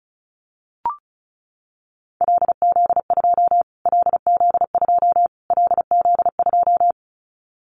Morse code, heard as a keyed tone, about 34 wpm, 700 Hz. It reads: LZ2 LZ2 LZ2
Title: Callsign Prefixes for Top 25 DXCC - Repeated 3x - 35wpm